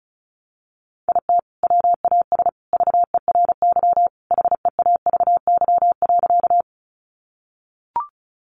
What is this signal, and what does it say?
Morse code, heard as a keyed tone, about 34 wpm, 700 Hz. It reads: IT WAS VERY HEAVY.
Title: Sentences from Top 1100 Words 35wpm